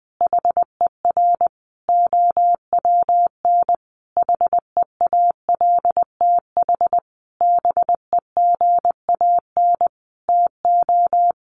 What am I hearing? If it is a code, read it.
Morse code, heard as a keyed tone, about 20 wpm, 700 Hz. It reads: HER OWN HEALTH BEGAN TO
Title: Sentences from Top 1100 Words 20wpm